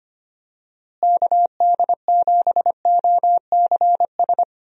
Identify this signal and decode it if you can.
Morse code, heard as a keyed tone, about 25 wpm, 700 Hz. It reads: KD7OCS
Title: USA Call Signs - 6 Characters - Repeated 3x - 25wpm